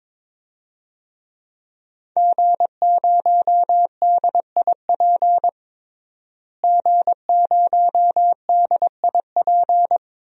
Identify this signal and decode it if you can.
Morse code, heard as a keyed tone, about 22 wpm, 700 Hz. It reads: G0DIP G0DIP
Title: DXCC-100 Call Signs - 5 Characters - Repeated 2x - 22wpm